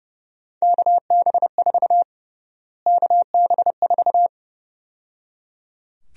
Morse code, heard as a keyed tone, about 30 wpm, 700 Hz. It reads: KB4 KB4
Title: USA Call Sign Prefixes - Repeated 2x - 30wpm